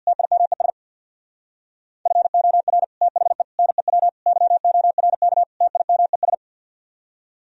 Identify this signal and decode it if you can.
Morse code, heard as a keyed tone, about 65 wpm, 700 Hz. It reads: TIMES FOR THE NEW YORK TIMES